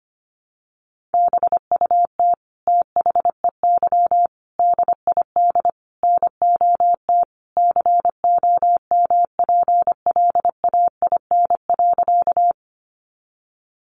Morse code, heard as a keyed tone, about 25 wpm, 700 Hz. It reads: BUT THEY DID NOT COMPLAIN.
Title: Sentences from Top 1300 Words 25wpm